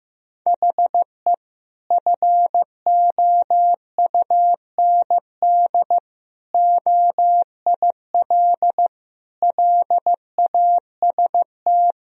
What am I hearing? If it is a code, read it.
Morse code, heard as a keyed tone, about 15 wpm, 700 Hz. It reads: HE FOUND OIL LAST
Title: Sentences from Top 1000 Words - Review Part 4 - 15wpm